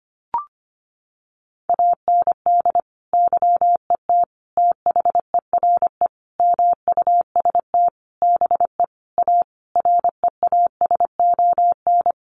Morse code, heard as a keyed tone, about 25 wpm, 700 Hz. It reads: AND YET THERE MUST BE A REASON
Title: Sentences from Top 1300 Words 25wpm